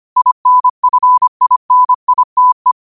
Morse code, heard as a keyed tone, about 25 wpm, 1000 Hz. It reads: INFINITE
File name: qotdfiles / 25wpm